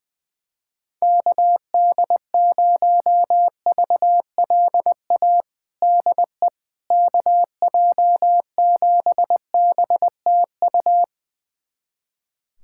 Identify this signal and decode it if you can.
Morse code, heard as a keyed tone, about 20 wpm, 700 Hz. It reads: KD0VLA DE KJ7BTU
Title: USA Callsign DE Callsign - 6 Characters - 20wpm